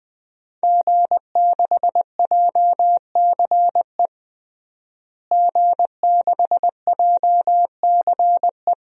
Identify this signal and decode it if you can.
Morse code, heard as a keyed tone, about 21 wpm, 700 Hz. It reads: G6JCE G6JCE
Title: DXCC-100 Call Signs - 5 Characters - Repeated 2x - 20wpm